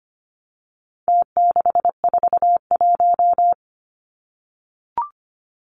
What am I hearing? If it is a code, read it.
Morse code, heard as a keyed tone, about 25 wpm, 700 Hz. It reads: T641
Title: Numbers - Contest Style - 4-Digit 25wpm